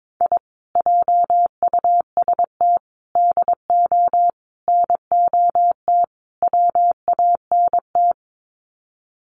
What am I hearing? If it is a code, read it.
Morse code, heard as a keyed tone, about 22 wpm, 700 Hz. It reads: I JUST DO NOT WANT
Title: Sets of 5 Words Encore2 22wpm